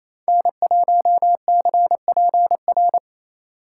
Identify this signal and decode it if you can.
Morse code, heard as a keyed tone, about 36 wpm, 700 Hz. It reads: N1CPR